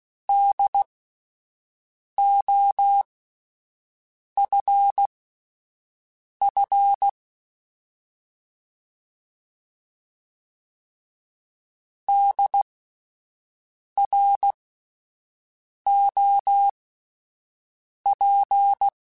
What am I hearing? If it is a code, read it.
Morse code, heard as a keyed tone, about 16 wpm, spaced as for 5 wpm, 800 Hz. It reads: DOFF DROP